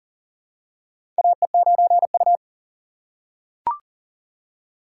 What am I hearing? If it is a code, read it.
Morse code, heard as a keyed tone, about 40 wpm, 700 Hz. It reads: AE9U
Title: USA Call Signs - 4 Characters - 2x Character Spacing - 40wpm